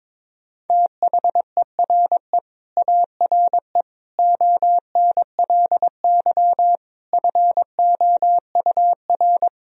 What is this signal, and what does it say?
Morse code, heard as a keyed tone, about 22 wpm, 700 Hz. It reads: THERE ARE ONLY FOUR
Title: Sentences from Top 1000 Words - Review Part 1 - 22wpm